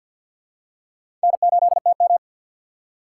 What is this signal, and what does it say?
Morse code, heard as a keyed tone, about 48 wpm, 700 Hz. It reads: N8TM
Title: USA Call Signs - 2x Character Spacing - 50wpm